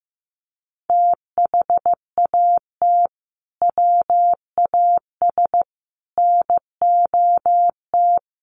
Morse code, heard as a keyed tone, about 15 wpm, 700 Hz. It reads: THAT WAS NOT
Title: Sentences from Top 1400 Words 15wpm